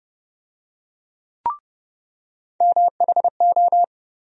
Morse code, heard as a keyed tone, about 30 wpm, 700 Hz. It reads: MHO
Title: USA Call Sign Postfixes - 3 Letters 30wpm